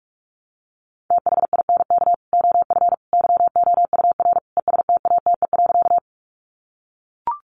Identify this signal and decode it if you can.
Morse code, heard as a keyed tone, about 44 wpm, 700 Hz. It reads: THINK OF YOUR ESTATE.
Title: Sentences from Top 1200 Words 45wpm